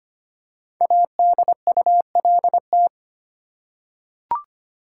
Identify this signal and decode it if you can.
Morse code, heard as a keyed tone, about 25 wpm, 700 Hz. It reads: ADULT
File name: 5 Letter Words - First 3 Letters Given - 25wpm